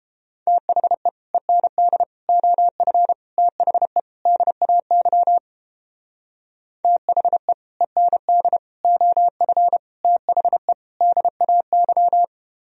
Morse code, heard as a keyed tone, about 29 wpm, 700 Hz. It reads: THE END OF THE DAY THE END OF THE DAY